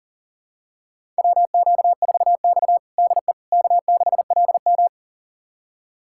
Morse code, heard as a keyed tone, about 40 wpm, 700 Hz. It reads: WQ4X DE K6LM
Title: USA Callsign DE Callsign - 4 Characters - 40wpm